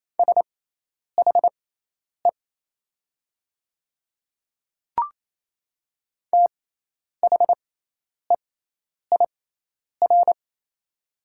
Morse code, heard as a keyed tone, about 27 wpm, spaced as for 10 wpm, 700 Hz. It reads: SHE THEIR
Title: 6x Character Spacing - Top 100 Words - 28wpm